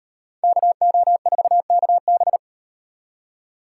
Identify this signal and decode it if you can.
Morse code, heard as a keyed tone, about 38 wpm, 700 Hz. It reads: KO4KB